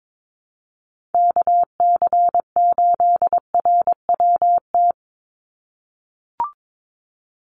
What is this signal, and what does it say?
Morse code, heard as a keyed tone, about 22 wpm, 700 Hz. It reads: KC8RWT
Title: USA Call Signs 22wpm